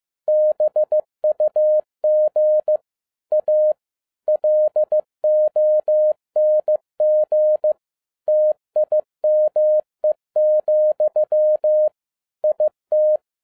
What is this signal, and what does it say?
Morse code, heard as a keyed tone, about 15 wpm, 610 Hz. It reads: BUG A LONG TIME, IT